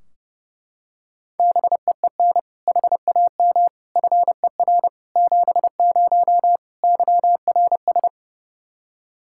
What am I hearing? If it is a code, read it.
Morse code, heard as a keyed tone, about 32 wpm, 700 Hz. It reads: BEEN HAM FER 70 YRS
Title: Practice QSO - Exchange 3 - More Info - 30wpm